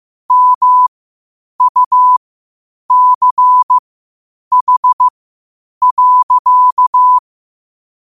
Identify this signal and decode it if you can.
Morse code, heard as a keyed tone, about 15 wpm, spaced as for 8 wpm, 1000 Hz. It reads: MUCH.
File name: qotdfiles / 8wpm